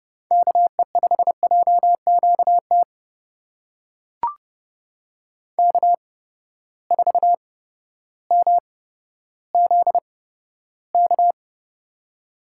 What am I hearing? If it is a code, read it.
Morse code, heard as a keyed tone, about 32 wpm, spaced as for 14 wpm, 700 Hz. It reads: KE5JQT K4MZK
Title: USA Call Signs - 8x Character Spacing - 30wpm